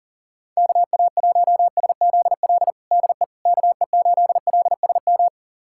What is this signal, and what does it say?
Morse code, heard as a keyed tone, about 40 wpm, 700 Hz. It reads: KA1SZL DE KE8LSM
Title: USA Callsign DE Callsign - 6 Characters - 40wpm